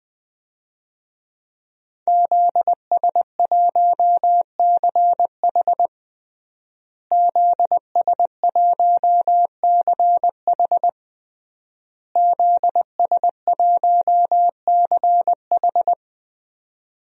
Morse code, heard as a keyed tone, about 20 wpm, 700 Hz. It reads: ZS1CH ZS1CH ZS1CH
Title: DXCC-100 Call Signs - 5 Characters - Repeated 3x - 20wpm